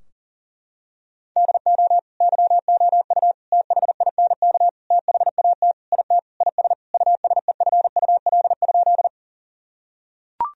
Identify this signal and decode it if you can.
Morse code, heard as a keyed tone, about 40 wpm, 700 Hz. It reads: DO YOU THINK THAT IT IS USEFUL?
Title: Sentences from Top 1000 Words - Review Part 4 - 40wpm